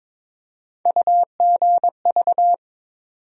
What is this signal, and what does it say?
Morse code, heard as a keyed tone, about 22 wpm, 700 Hz. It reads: UGV
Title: USA Call Sign Postfixes - 3 Letters 22wpm